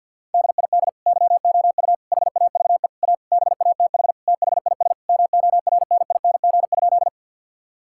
Morse code, heard as a keyed tone, about 50 wpm, 700 Hz. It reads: DID YOU HAVE A BATH THIS MORNING?
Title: Sentences from Top 1600 Words 50wpm